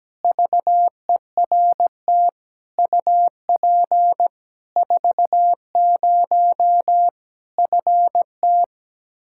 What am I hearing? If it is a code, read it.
Morse code, heard as a keyed tone, about 17 wpm, 700 Hz. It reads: VERT UP 40 FT
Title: K1USN SST QSO Practice 17wpm